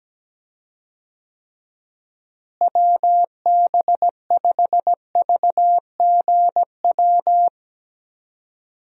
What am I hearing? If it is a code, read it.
Morse code, heard as a keyed tone, about 17 wpm, 700 Hz. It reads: WB5VGW
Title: USA Call Signs - 6 Characters - Repeated 3x - 17wpm